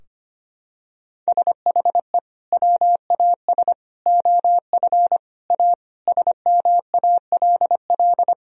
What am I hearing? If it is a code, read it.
Morse code, heard as a keyed tone, about 25 wpm, 700 Hz. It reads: SHE WAS OF A SMALL
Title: Sentences from Top 1000 Words - Review Part 2 - 25wpm